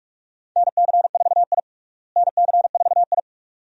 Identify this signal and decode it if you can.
Morse code, heard as a keyed tone, about 46 wpm, 700 Hz. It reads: NC4I NC4I
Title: USA Call Signs - 4 Characters - Repeated 2x - 45wpm